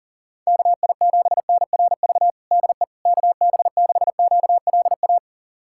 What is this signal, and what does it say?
Morse code, heard as a keyed tone, about 40 wpm, 700 Hz. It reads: KI7NRV DE KB6QLA
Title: USA Callsign DE Callsign 40wpm